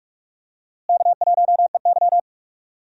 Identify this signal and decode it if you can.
Morse code, heard as a keyed tone, about 46 wpm, 700 Hz. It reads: K1EY
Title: USA Call Signs - 4 Characters - Repeated 2x - 45wpm